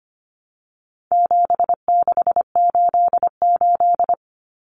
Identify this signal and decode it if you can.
Morse code, heard as a keyed tone, about 25 wpm, 700 Hz. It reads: 7688
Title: Common Cut Numbers - 4-Digit 25wpm